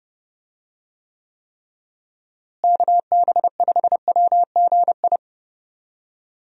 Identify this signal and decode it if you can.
Morse code, heard as a keyed tone, about 32 wpm, 700 Hz. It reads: KB5WGI
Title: USA Call Signs - 6 Characters - Repeated 3x - 30wpm